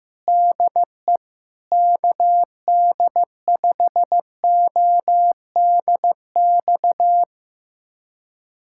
Morse code, heard as a keyed tone, about 15 wpm, 700 Hz. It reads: DE KD5ODX
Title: USA Callsign DE Callsign - 6 Characters - 15wpm